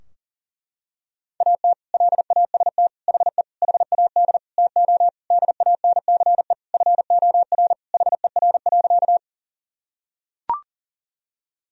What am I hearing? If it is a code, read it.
Morse code, heard as a keyed tone, about 40 wpm, 700 Hz. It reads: AT LAST HE HAD TO DANCE FOR HER.
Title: Sentences from Top 1000 Words - Review Part 4 - 40wpm